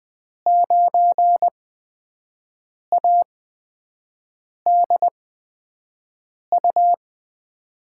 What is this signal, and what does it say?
Morse code, heard as a keyed tone, about 20 wpm, spaced as for 5 wpm, 700 Hz. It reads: 9ADU
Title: USA Call Signs - 5 Characters - 8x Character Spacing - 20wpm